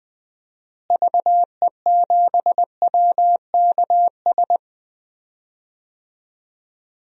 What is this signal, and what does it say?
Morse code, heard as a keyed tone, about 21 wpm, 700 Hz. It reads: VE7WKS
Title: DXCC-100 Call Signs - 6 Characters - Repeated 3x - 20wpm